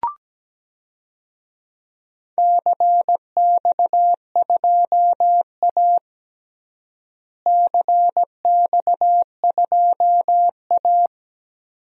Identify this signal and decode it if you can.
Morse code, heard as a keyed tone, about 17 wpm, 700 Hz. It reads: CX2A CX2A
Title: DXCC-100 Call Signs - 4 Characters - Repeated 3x - 17wpm